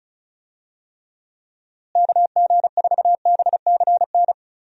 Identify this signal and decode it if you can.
Morse code, heard as a keyed tone, about 34 wpm, 700 Hz. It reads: KG4BCN
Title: USA Call Signs - 6 Characters - Repeated 3x - 35wpm